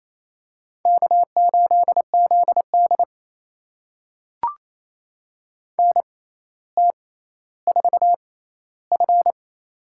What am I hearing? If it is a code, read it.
Morse code, heard as a keyed tone, about 28 wpm, spaced as for 14 wpm, 700 Hz. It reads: K8ZD NT4F
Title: USA Call Signs - 4 Characters - 6x Character Spacing - 28wpm